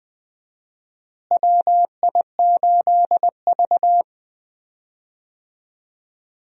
Morse code, heard as a keyed tone, about 20 wpm, 700 Hz. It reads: WI8V
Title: USA Call Signs - 4 Characters - Repeated 3x - 20wpm